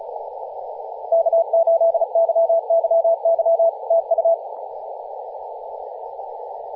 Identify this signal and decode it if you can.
Morse code, heard as a keyed tone, about 35 wpm, 670 Hz, with radio noise. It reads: K8YYY TU
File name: posts / 2016-nov-11-2050z-tl8ao-10-112mhz-cw